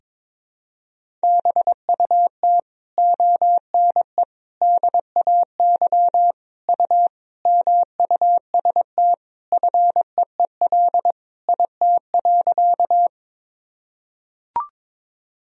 Morse code, heard as a keyed tone, about 22 wpm, 700 Hz. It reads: BUT ONE DAY U MUST FEEL IT.
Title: Sentences from Top 400 Words and Abbreviations 22wpm